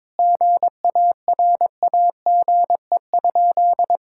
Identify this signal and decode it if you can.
Morse code, heard as a keyed tone, about 22 wpm, 700 Hz. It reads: GARAGE?
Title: Sentences from Top 1600 Words 22wpm